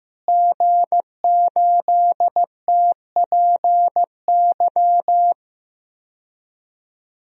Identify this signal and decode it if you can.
Morse code, heard as a keyed tone, about 15 wpm, 700 Hz. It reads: G8TPY
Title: DXCC-100 Call Signs - 5 Characters - Repeated 3x - 15wpm